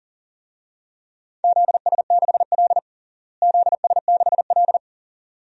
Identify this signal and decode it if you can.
Morse code, heard as a keyed tone, about 40 wpm, 700 Hz. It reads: ZS6L ZS6L
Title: DXCC-100 Call Signs - 4 Characters - Repeated 2x - 40wpm